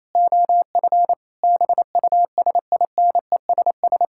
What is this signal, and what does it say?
Morse code, heard as a keyed tone, about 28 wpm, 700 Hz. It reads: OF BUSINESS
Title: Sentences from Top 1000 Words - Review Part 6 - 28wpm